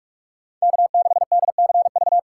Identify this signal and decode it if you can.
Morse code, heard as a keyed tone, about 48 wpm, 700 Hz. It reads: K6DCV
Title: USA Call Signs - 5 Characters - 2x Character Spacing - 45wpm